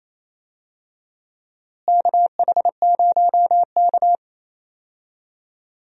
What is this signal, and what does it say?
Morse code, heard as a keyed tone, about 28 wpm, 700 Hz. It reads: KH0K
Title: USA Call Signs - 4 Characters - Repeated 2x - 28wpm